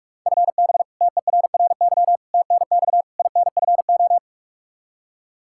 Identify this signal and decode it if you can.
Morse code, heard as a keyed tone, about 42 wpm, 700 Hz. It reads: FB TERRY TNX INFO
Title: Practice QSO - Exchange 3 - More Info - 45wpm